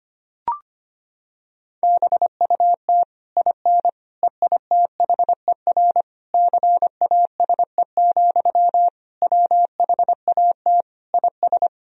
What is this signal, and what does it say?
Morse code, heard as a keyed tone, about 25 wpm, 700 Hz. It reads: BUT IN EITHER CASE, WHAT IS
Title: Sentences from Top 1000 Words - Review Part 4 - 25wpm